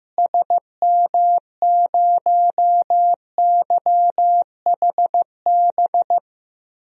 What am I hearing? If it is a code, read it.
Morse code, heard as a keyed tone, about 15 wpm, 700 Hz. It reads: SM0YHB
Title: DXCC-100 Call Signs - 6 Characters - Repeated 2x - 15wpm